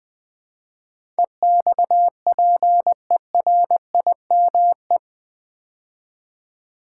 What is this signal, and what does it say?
Morse code, heard as a keyed tone, about 20 wpm, 700 Hz. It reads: EXPERIME
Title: Build A Word - 4 Letters At A Time - 12-Letter Words - 20wpm